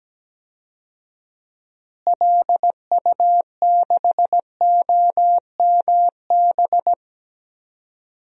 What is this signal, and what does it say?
Morse code, heard as a keyed tone, about 17 wpm, 700 Hz. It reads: LU6OMB
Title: DXCC-100 Call Signs - 6 Characters - Repeated 3x - 17wpm